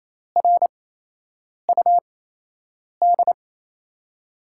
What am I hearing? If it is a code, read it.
Morse code, heard as a keyed tone, about 28 wpm, spaced as for 8 wpm, 700 Hz. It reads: RUD
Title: USA Call Signs - 8x Character Spacing - 28wpm